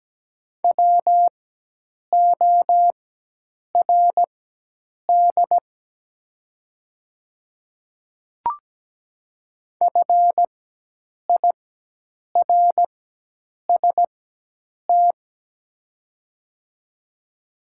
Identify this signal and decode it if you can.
Morse code, heard as a keyed tone, about 17 wpm, spaced as for 8 wpm, 700 Hz. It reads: WORD FIRST
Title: 4x Character Spacing - Top 500 Words - 17wpm